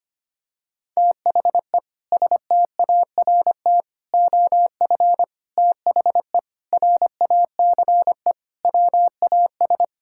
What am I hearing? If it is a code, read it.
Morse code, heard as a keyed tone, about 25 wpm, 700 Hz. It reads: THE START OF THE RACE WAS
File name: practice / 25wpm / Sentences with 4-Word Phrases - 25wpm